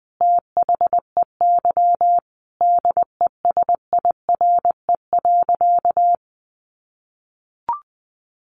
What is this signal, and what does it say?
Morse code, heard as a keyed tone, about 20 wpm, 700 Hz. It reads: THEY DESIRE.
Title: Sentences from Top 1100 Words 20wpm